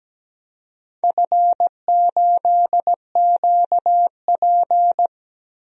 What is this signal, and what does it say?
Morse code, heard as a keyed tone, about 17 wpm, 700 Hz. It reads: F8QP
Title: DXCC-100 Call Signs - 4 Characters - Repeated 3x - 17wpm